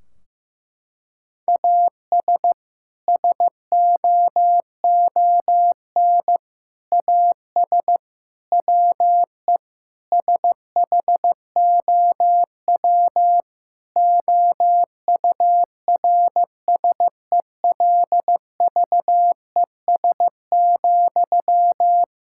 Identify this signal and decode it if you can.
Morse code, heard as a keyed tone, about 15 wpm, 700 Hz. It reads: AS SOON AS WE SHOW OURSELVES,
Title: Sentences from Top 1600 Words 15wpm